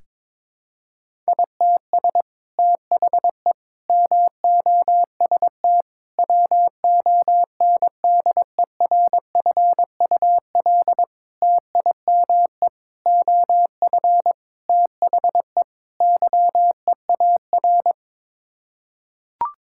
Morse code, heard as a keyed tone, about 22 wpm, 700 Hz. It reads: ITS THE MOST WONDERFUL TIME OF THE YEAR
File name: practice / 22wpm / Sets of 8 Words - 22wpm